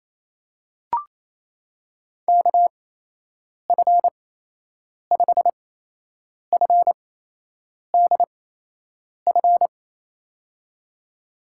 Morse code, heard as a keyed tone, about 28 wpm, spaced as for 8 wpm, 700 Hz. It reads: KF5FDF
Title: USA Call Signs - 6 Characters - 8x Character Spacing - 28wpm